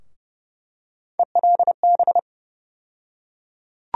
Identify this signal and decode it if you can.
Morse code, heard as a keyed tone, about 30 wpm, 700 Hz. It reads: ELB
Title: USA Call Sign Postfixes - 3 Letters 30wpm